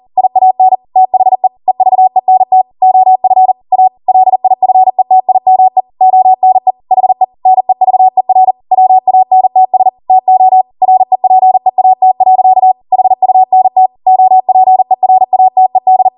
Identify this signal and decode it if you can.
Morse code, heard as a keyed tone, about 40 wpm, 750 Hz. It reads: IAN THE EVENT OF A LIFETIME ONE HE NEVER WANTS TO REPEAT. HUNT OPERATED